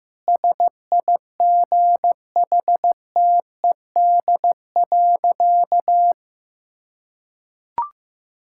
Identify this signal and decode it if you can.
Morse code, heard as a keyed tone, about 15 wpm, 700 Hz. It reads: SIGHTED.
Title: Pangram Challenge v2 - 15wpm